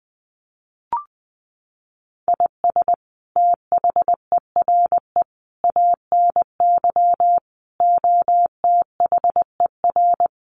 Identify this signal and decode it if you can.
Morse code, heard as a keyed tone, about 20 wpm, 700 Hz. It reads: IS THERE ANY OTHER
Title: Sentences from Top 1100 Words 20wpm